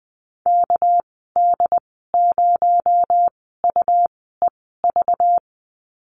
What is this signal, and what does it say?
Morse code, heard as a keyed tone, about 20 wpm, spaced as for 14 wpm, 700 Hz. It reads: KD0UEV
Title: 231 Lesson 37g - Letter x - Call Signs 20wpm